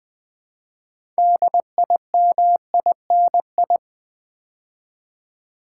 Morse code, heard as a keyed tone, about 20 wpm, 700 Hz. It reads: DIMINI
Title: Build A Word - 2 Letters At A Time - 8-Letter Words - 20wpm